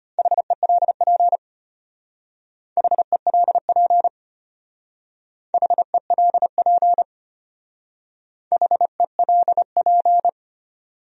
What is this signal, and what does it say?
Morse code, heard as a keyed tone, about 32 wpm, 700 Hz. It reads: HELP HELP HELP HELP